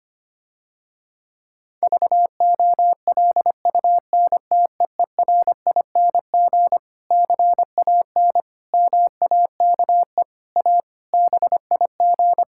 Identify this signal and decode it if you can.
Morse code, heard as a keyed tone, about 25 wpm, 700 Hz. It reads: VOLUNTEERING CAN MAKE A BIG
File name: practice / 25wpm / Difficult Sentences with Hints - 25wpm